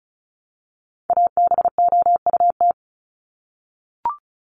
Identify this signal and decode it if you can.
Morse code, heard as a keyed tone, about 34 wpm, 700 Hz. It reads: ABOUT
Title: Top 500 Words 35wpm